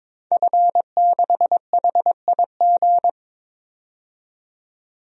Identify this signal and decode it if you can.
Morse code, heard as a keyed tone, about 22 wpm, 700 Hz. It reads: F6HIG
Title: DXCC-100 Call Signs - 5 Characters - 22wpm